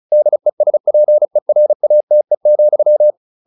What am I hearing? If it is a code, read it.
Morse code, heard as a keyed tone, about 34 wpm, 600 Hz. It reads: DESPERATE,
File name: the old man and the sea 2 35wpm